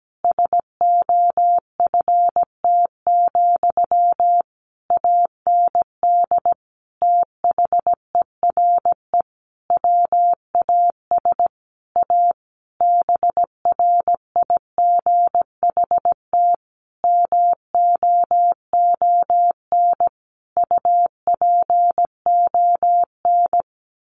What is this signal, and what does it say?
Morse code, heard as a keyed tone, about 17 wpm, 700 Hz. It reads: SOFT, AND THERE WAS A BRIGHT MOON UPON
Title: Sentences from Top 1500 Words 17wpm